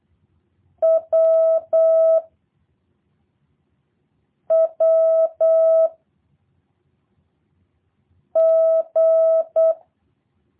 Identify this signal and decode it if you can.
Morse code, heard as a keyed tone, about 8 wpm, 650 Hz. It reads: W W G
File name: jh8cbh / 2023cw / review1-4